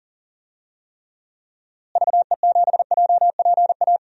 Morse code, heard as a keyed tone, about 37 wpm, 700 Hz. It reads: VE7JPA
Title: DXCC-100 Call Signs - 6 Characters - Repeated 2x - 40wpm